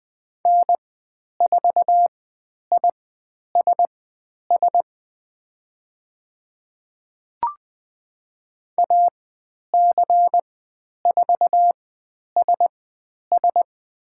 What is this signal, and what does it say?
Morse code, heard as a keyed tone, about 20 wpm, spaced as for 10 wpm, 700 Hz. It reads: N4ISS AC4SS
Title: 109 Lesson 18g - Letter m - Call Signs 20wpm